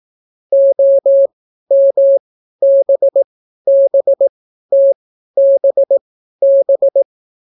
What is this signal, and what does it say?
Morse code, heard as a keyed tone, about 18 wpm, spaced as for 12 wpm, 550 Hz. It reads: OMBBTBB